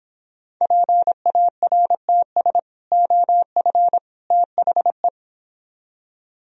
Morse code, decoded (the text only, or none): PARTS OF THE